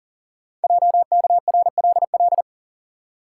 Morse code, heard as a keyed tone, about 40 wpm, 700 Hz. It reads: JKRLL
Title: Code Groups - 5 Letters - 40wpm